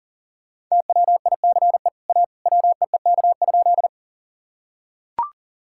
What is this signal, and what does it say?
Morse code, heard as a keyed tone, about 40 wpm, 700 Hz. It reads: TWICE A WEEK?
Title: Sentences from Top 1600 Words 40wpm